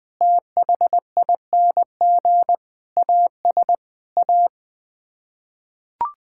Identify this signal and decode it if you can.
Morse code, heard as a keyed tone, about 20 wpm, 700 Hz. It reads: THING AS A
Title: Sets of 8 Words - 20wpm